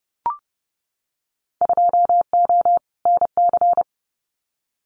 Morse code, heard as a keyed tone, about 30 wpm, 700 Hz. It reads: 2O NC
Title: Winter Field Day Exchanges 30wpm